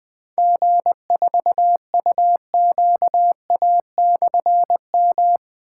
Morse code, heard as a keyed tone, about 20 wpm, 700 Hz. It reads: G4UQA/M
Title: Difficult Call Signs 20wpm